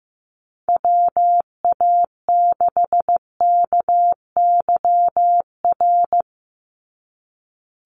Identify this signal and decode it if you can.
Morse code, heard as a keyed tone, about 15 wpm, 700 Hz. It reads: WA6KYR
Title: NA SOTA Chaser Callsigns 15wpm